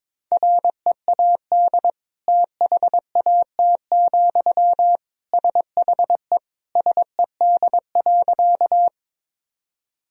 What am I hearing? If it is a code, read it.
Morse code, heard as a keyed tone, about 22 wpm, 700 Hz. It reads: READ THAT, SHE SED.
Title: Sentences from Top 200 Words and Abbreviations 22wpm